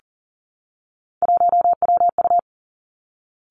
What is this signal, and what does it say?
Morse code, heard as a keyed tone, about 39 wpm, 700 Hz. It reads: 1WU
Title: Three Challenging Letter-Number Combinations 40wpm